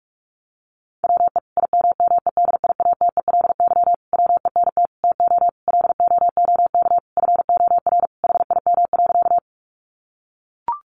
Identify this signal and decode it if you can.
Morse code, heard as a keyed tone, about 44 wpm, 700 Hz. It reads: WE IMMEDIATELY WENT TO LOOK FOR HIM.
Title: Sentences from Top 1100 Words 45wpm